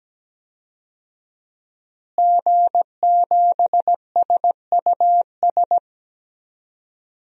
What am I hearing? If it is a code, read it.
Morse code, heard as a keyed tone, about 17 wpm, 700 Hz. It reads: G7SUS
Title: DXCC-100 Call Signs - 5 Characters - Repeated 3x - 17wpm